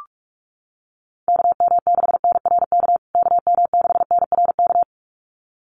Morse code, heard as a keyed tone, about 44 wpm, 700 Hz. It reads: KM6NRK KM6NRK
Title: USA Call Signs Repeated 2x 45wpm